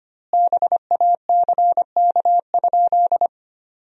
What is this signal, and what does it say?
Morse code, heard as a keyed tone, about 25 wpm, 700 Hz. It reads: BACK?
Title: 2x Word-Spacing - Sentences from Top 200 Words - 25wpm